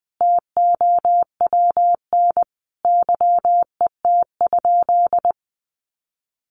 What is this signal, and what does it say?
Morse code, heard as a keyed tone, about 20 wpm, 700 Hz. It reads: TOWN YET?
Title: Sentences from Top 1500 Words 20wpm